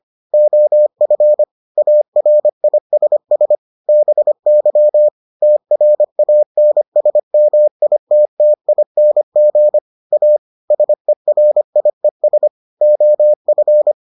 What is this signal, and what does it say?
Morse code, heard as a keyed tone, about 25 wpm, 600 Hz. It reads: OF ARISS BY TRANSMITTING A SERIES OF